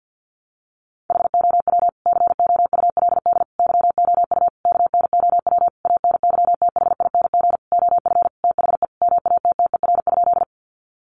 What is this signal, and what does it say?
Morse code, heard as a keyed tone, about 50 wpm, 700 Hz. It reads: HOW COULD YOU KNOW ANYTHING OF THE MATTER?
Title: Sentences from Top 1000 Words 50wpm